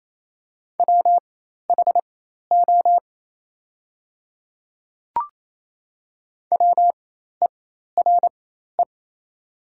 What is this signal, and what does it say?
Morse code, heard as a keyed tone, about 28 wpm, spaced as for 13 wpm, 700 Hz. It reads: WHO WERE
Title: 4x Character Spacing - Top 100 Words - 28wpm